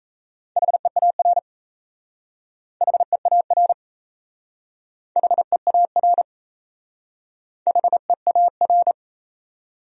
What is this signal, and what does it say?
Morse code, heard as a keyed tone, about 36 wpm, 700 Hz. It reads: HEAR HEAR HEAR HEAR